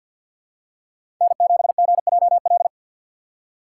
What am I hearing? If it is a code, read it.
Morse code, heard as a keyed tone, about 51 wpm, 700 Hz. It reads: N7GJL
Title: USA Call Signs - 3x Character Spacing - 50wpm